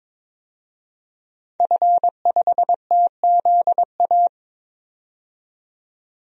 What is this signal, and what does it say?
Morse code, heard as a keyed tone, about 21 wpm, 700 Hz. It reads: F5TZA
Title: DXCC-100 Call Signs - 5 Characters - 22wpm